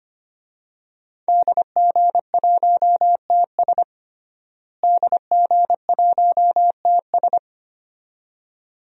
Morse code, heard as a keyed tone, about 25 wpm, 700 Hz. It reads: DG1TS DG1TS
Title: DXCC-100 Call Signs - 5 Characters - Repeated 2x - 25wpm